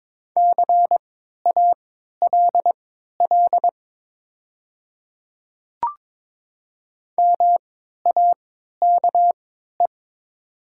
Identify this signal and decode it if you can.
Morse code, heard as a keyed tone, about 23 wpm, spaced as for 13 wpm, 700 Hz. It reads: CALL MAKE
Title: 3x Character Spacing - Top 300 Words - 22wpm